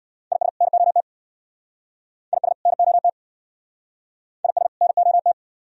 Morse code, heard as a keyed tone, about 58 wpm, 690 Hz. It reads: IS NOT IS NOT IS NOT
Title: Speed-Racing - Sets of 2 Words 50wpm